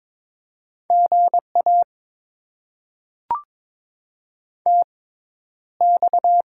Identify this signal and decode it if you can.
Morse code, heard as a keyed tone, about 22 wpm, spaced as for 11 wpm, 700 Hz. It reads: GA TX
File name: Top 90 QSO Elements - 6x Character Spacing - 22wpm